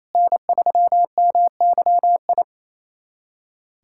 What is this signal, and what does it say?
Morse code, heard as a keyed tone, about 28 wpm, 700 Hz. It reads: N3MYI
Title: USA Callsign DE Callsign - 5 Characters - 28wpm